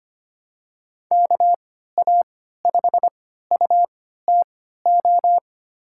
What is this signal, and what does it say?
Morse code, heard as a keyed tone, about 25 wpm, spaced as for 14 wpm, 700 Hz. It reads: KA5UTO
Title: USA Call Signs - 6 Characters - 3x Character Spacing - 25wpm